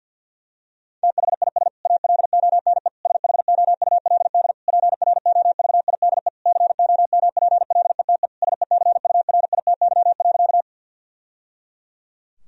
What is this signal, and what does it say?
Morse code, heard as a keyed tone, about 51 wpm, 700 Hz. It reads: THIS ALONE SHOULD PROVIDE COMPLETE SECURITY.
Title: Sentences from Top 1000 Words - Review Part 4 - 50wpm